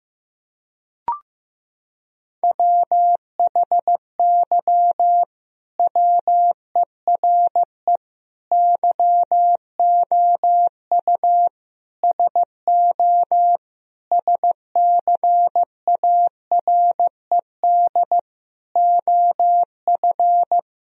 Morse code, heard as a keyed tone, about 15 wpm, 700 Hz. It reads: WHY WERE YOU SO SCARED OF
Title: Sentences from Top 1000 Words - Review Part 4 - 15wpm